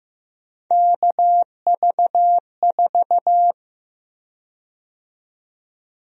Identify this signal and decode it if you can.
Morse code, heard as a keyed tone, about 16 wpm, 700 Hz. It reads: KV4
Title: USA Call Sign Prefixes - 15wpm